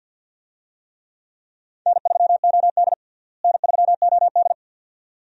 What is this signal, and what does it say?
Morse code, heard as a keyed tone, about 51 wpm, 700 Hz. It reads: N3OD N3OD
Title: USA Call Signs - 4 Characters - Repeated 2x - 50wpm